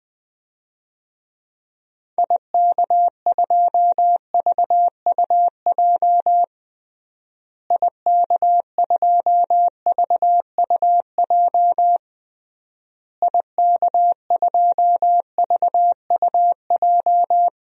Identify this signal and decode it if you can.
Morse code, heard as a keyed tone, about 20 wpm, 700 Hz. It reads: IK2VUJ IK2VUJ IK2VUJ
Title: DXCC-100 Call Signs - 6 Characters - Repeated 3x - 20wpm